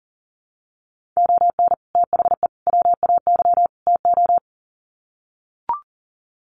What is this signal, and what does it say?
Morse code, heard as a keyed tone, about 40 wpm, 700 Hz. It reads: ON THE WAY TO
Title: Sets of 4 Words 40wpm